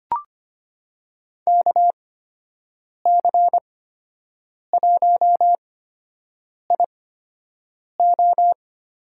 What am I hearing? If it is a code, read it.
Morse code, heard as a keyed tone, about 25 wpm, spaced as for 7 wpm, 700 Hz. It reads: KC1IO
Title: USA Call Signs - 6 Characters - 8x Character Spacing - 25wpm